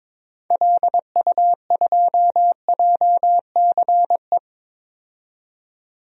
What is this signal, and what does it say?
Morse code, heard as a keyed tone, about 22 wpm, 700 Hz. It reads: LU2JCE
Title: DXCC-100 Call Signs - 6 Characters - 22wpm